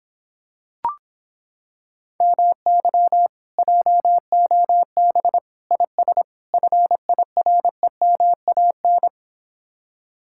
Morse code, heard as a keyed tone, about 26 wpm, 700 Hz. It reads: MY JOB IS FIREMAN